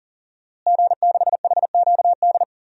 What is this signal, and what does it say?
Morse code, heard as a keyed tone, about 40 wpm, 700 Hz. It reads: G6HQD
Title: DXCC-100 Call Signs - 5 Characters - Repeated 3x - 40wpm